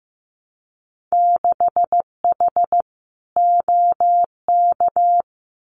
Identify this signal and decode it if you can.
Morse code, heard as a keyed tone, about 15 wpm, 700 Hz. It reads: 6H OK
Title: Winter Field Day Exchanges Encore 15wpm